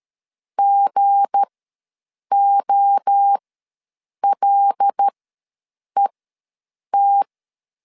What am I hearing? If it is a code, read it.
Morse code, heard as a keyed tone, about 13 wpm, spaced as for 7 wpm, 800 Hz. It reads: GOLET
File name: LEZIONI / 7 GF9Z / 7 G ws 12wpm 3140